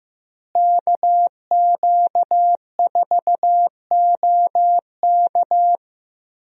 Morse code, heard as a keyed tone, about 15 wpm, 700 Hz. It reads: KQ4OK
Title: USA Call Signs - 5 Characters - Repeated 3x - 15wpm